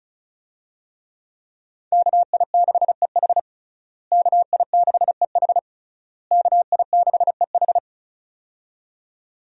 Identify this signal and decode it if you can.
Morse code, heard as a keyed tone, about 34 wpm, 700 Hz. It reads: KI6EH KI6EH KI6EH
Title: USA Call Signs - 5 Characters - Repeated 3x - 35wpm